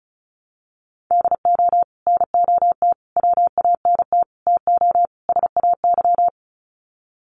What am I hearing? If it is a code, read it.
Morse code, heard as a keyed tone, about 34 wpm, 700 Hz. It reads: DO NOT WANT TO SAY
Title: Sets of 5 Words Encore 35wpm